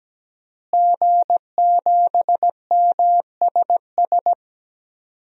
Morse code, heard as a keyed tone, about 17 wpm, 700 Hz. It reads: G7MSS
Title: DXCC-100 Call Signs - 5 Characters - Repeated 2x - 17wpm